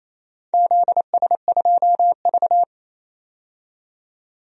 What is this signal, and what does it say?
Morse code, heard as a keyed tone, about 28 wpm, 700 Hz. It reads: ZS2V
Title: DXCC-100 Call Signs - 4 Characters - 28wpm